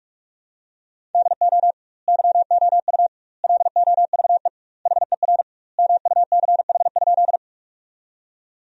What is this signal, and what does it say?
Morse code, heard as a keyed tone, about 47 wpm, 700 Hz. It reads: DO YOU LOVE HER MUCH?
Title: 2x Word-Spacing - Sentences from Top 400 Words - 45wpm